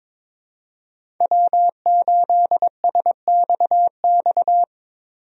Morse code, heard as a keyed tone, about 22 wpm, 700 Hz. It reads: W8SXX
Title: USA Call Signs - 5 Characters - Repeated 3x - 22wpm